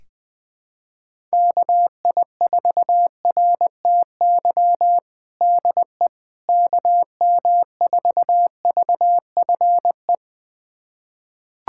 Morse code, heard as a keyed tone, about 21 wpm, 700 Hz. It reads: KI4RTY DE KM4VFE
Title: USA Callsign DE Callsign - 6 Characters - 20wpm